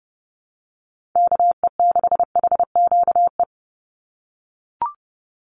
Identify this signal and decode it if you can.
Morse code, heard as a keyed tone, about 30 wpm, 700 Hz. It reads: KE6HQE
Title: USA Call Signs 30wpm